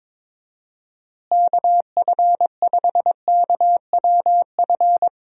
Morse code, heard as a keyed tone, about 21 wpm, 700 Hz. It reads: KF5KWF
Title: USA Call Signs - 6 Characters - Repeated 2x - 22wpm